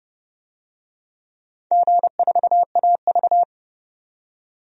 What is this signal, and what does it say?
Morse code, heard as a keyed tone, about 28 wpm, 700 Hz. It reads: G4AV
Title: DXCC-100 Call Signs - 4 Characters - 30wpm